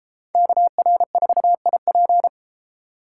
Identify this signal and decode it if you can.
Morse code, heard as a keyed tone, about 33 wpm, 700 Hz. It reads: KR4IP